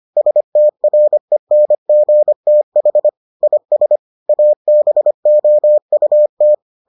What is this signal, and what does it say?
Morse code, heard as a keyed tone, about 25 wpm, 600 Hz. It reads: STRENGTH IS ABOUT